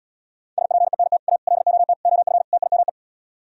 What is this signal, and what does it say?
Morse code, heard as a keyed tone, about 78 wpm, 690 Hz. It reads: SPENT A LOT OF TIME